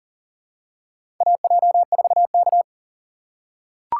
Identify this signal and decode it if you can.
Morse code, heard as a keyed tone, about 40 wpm, 700 Hz. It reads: AJ4K
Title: USA Call Signs - 4 Characters - 2x Character Spacing - 40wpm